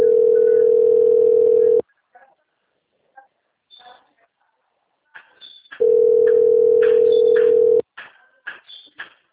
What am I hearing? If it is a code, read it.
Ringback tone (North America), 440 + 480 Hz.